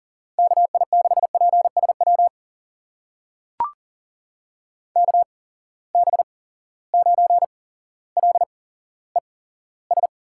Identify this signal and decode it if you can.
Morse code, heard as a keyed tone, about 39 wpm, spaced as for 17 wpm, 700 Hz. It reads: KI6PSW KB9LES
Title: USA Call Signs - 6 Characters - 8x Character Spacing - 40wpm